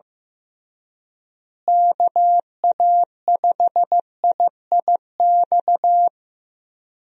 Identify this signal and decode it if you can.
Morse code, heard as a keyed tone, about 15 wpm, 700 Hz. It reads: KA5IIX
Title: USA Call Signs - 6 Characters - Repeated 3x - 15wpm